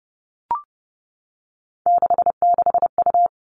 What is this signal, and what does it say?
Morse code, heard as a keyed tone, about 30 wpm, 700 Hz. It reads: 66U
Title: All Cut Numbers - 3-Digit 30wpm